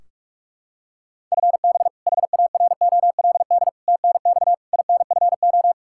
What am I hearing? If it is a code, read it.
Morse code, heard as a keyed tone, about 46 wpm, 700 Hz. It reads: FB HAROLD TNX INFO
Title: Practice QSO - Exchange 3 - More Info - 45wpm